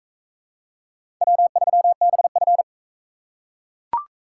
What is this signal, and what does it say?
Morse code, heard as a keyed tone, about 41 wpm, 700 Hz. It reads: W3BF